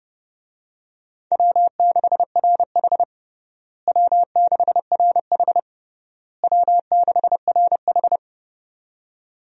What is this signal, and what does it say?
Morse code, heard as a keyed tone, about 30 wpm, 700 Hz. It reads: W6RH W6RH W6RH
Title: USA Call Signs - 4 Characters - Repeated 3x - 30wpm